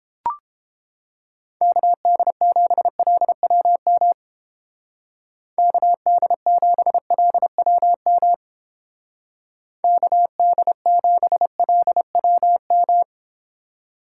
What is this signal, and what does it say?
Morse code, heard as a keyed tone, about 28 wpm, 700 Hz. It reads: KD7LWM KD7LWM KD7LWM